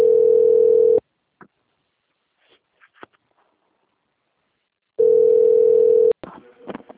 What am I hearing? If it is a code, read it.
Ringback tone (North America), 440 + 480 Hz.
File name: rg-1600-unknown- - - .7152